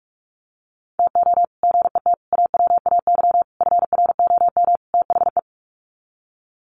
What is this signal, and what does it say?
Morse code, heard as a keyed tone, about 44 wpm, 700 Hz. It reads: TO GET AWAY FROM THE
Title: Sets of 5 Words Encore2 45wpm